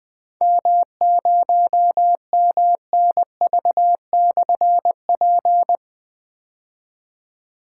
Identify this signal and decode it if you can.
Morse code, heard as a keyed tone, about 20 wpm, 700 Hz. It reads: M0MNV/P
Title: Difficult Call Signs 20wpm